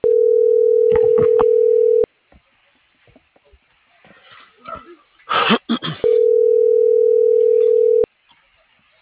Ringback tone (North America), 440 + 480 Hz.